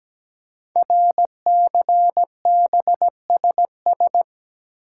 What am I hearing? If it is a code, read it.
Morse code, heard as a keyed tone, about 17 wpm, 700 Hz. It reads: RCBSS
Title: Code Groups - 5 Letters - 17wpm